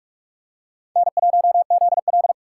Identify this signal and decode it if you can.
Morse code, heard as a keyed tone, about 42 wpm, 700 Hz. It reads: N1ZL